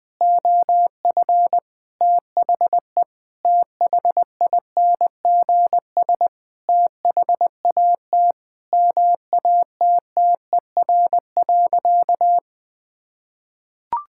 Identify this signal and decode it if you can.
Morse code, heard as a keyed tone, about 20 wpm, 700 Hz. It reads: OF THE THINGS THAT MATTER.
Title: Sentences with 4-Word Phrases - 20wpm